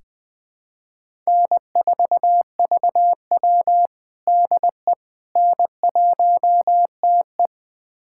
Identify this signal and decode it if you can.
Morse code, heard as a keyed tone, about 20 wpm, 700 Hz. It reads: N4VW DE N1TE
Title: USA Callsign DE Callsign - 4 Characters - 20wpm